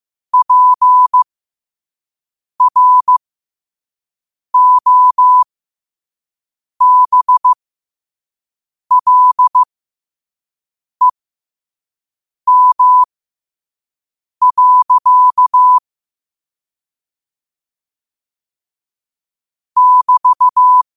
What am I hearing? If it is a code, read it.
Morse code, heard as a keyed tone, about 15 wpm, spaced as for 5 wpm, 1000 Hz. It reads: PROBLEM. <BT>
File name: qotdfiles / 5wpm